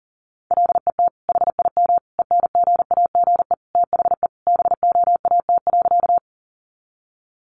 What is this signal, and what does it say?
Morse code, heard as a keyed tone, about 40 wpm, 700 Hz. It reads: LET HIM ENGAGE THE BOAT.
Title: Sentences from Top 1500 Words 40wpm